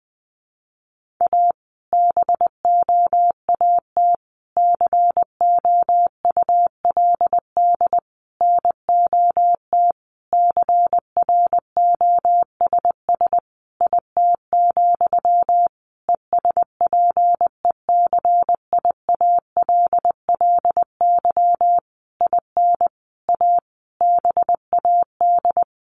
Morse code, heard as a keyed tone, about 20 wpm, 700 Hz. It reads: A BOAT COULD NOT CROSS IT, ESPECIALLY IN A BAD
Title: Sentences from Top 1400 Words 20wpm